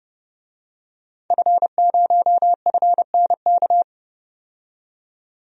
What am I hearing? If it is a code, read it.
Morse code, heard as a keyed tone, about 30 wpm, 700 Hz. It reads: F0FNK
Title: DXCC-100 Call Signs - 5 Characters - 30wpm